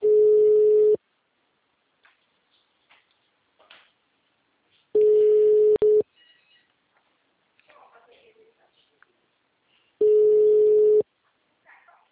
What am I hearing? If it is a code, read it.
Ringback tone (Europe, 425 Hz), with speech.
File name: out- -2002- - - .33